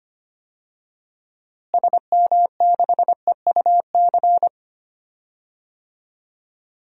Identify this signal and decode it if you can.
Morse code, heard as a keyed tone, about 25 wpm, 700 Hz. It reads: SM6EUC
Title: DXCC-100 Call Signs - 6 Characters - Repeated 2x - 25wpm